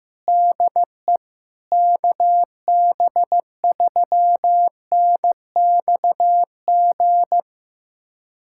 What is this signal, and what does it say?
Morse code, heard as a keyed tone, about 16 wpm, 700 Hz. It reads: DE KB3NXG
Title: USA Callsign DE Callsign - 6 Characters - 15wpm